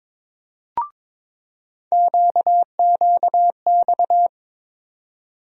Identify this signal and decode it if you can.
Morse code, heard as a keyed tone, about 22 wpm, 700 Hz. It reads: QQX
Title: USA Call Sign Postfixes - 3 Letters 22wpm